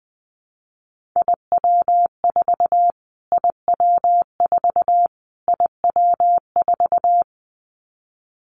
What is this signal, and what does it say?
Morse code, heard as a keyed tone, about 20 wpm, 700 Hz. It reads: IW4 IW4 IW4
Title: Callsign Prefixes for Top 25 DXCC - Repeated 3x - 20wpm